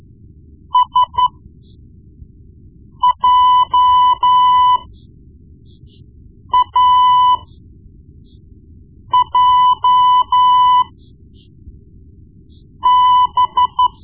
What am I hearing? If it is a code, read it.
Morse code, heard as a keyed tone, about 13 wpm, spaced as for 4 wpm, 1000 Hz. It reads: SJAJB
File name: Morse Lessons / Morse Lessons / Tape-1B